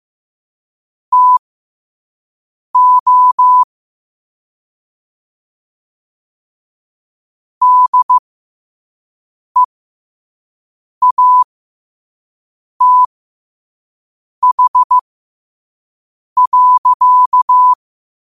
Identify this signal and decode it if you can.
Morse code, heard as a keyed tone, about 15 wpm, spaced as for 5 wpm, 1000 Hz. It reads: TO DEATH.